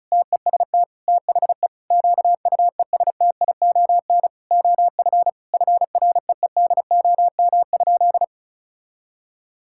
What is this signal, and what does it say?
Morse code, heard as a keyed tone, about 36 wpm, 700 Hz. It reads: TEST THE QUESTION OF FREEDOM?
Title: Sentences from Top 1600 Words 35wpm